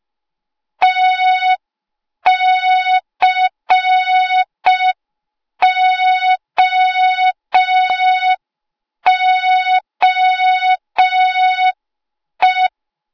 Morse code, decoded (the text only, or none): TCOOE